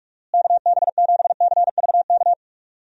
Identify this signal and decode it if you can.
Morse code, heard as a keyed tone, about 46 wpm, 700 Hz. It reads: KB7CVK
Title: USA Call Signs - 6 Characters - Repeated 2x - 45wpm